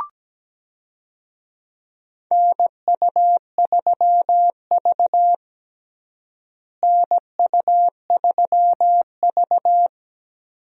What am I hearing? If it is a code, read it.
Morse code, heard as a keyed tone, about 17 wpm, 700 Hz. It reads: NU3V NU3V
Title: USA Call Signs - 4 Characters - Repeated 3x - 17wpm